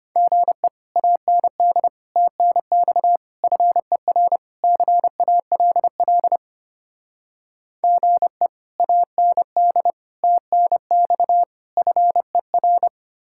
Morse code, heard as a keyed tone, about 32 wpm, 700 Hz. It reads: GE AND TNX FER CALL GE AND TNX FER